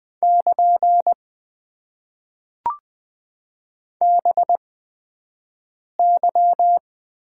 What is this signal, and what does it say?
Morse code, heard as a keyed tone, about 20 wpm, spaced as for 5 wpm, 700 Hz. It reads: <KN> BY
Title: Top 90 QSO Elements - 8x Character Spacing - 20wpm